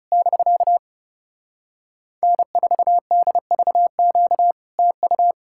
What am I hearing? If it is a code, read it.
Morse code, heard as a keyed tone, about 37 wpm, 700 Hz. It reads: <BK> N4DVQ TU
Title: Speed-Racing Common QSO Phrases - 30wpm